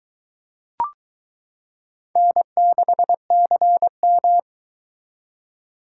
Morse code, heard as a keyed tone, about 22 wpm, 700 Hz. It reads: N6CM